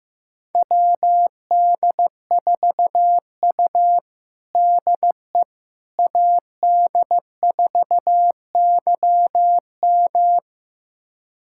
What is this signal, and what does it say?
Morse code, heard as a keyed tone, about 15 wpm, 700 Hz. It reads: WD4U DE AD4YM
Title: USA Callsign DE Callsign 15wpm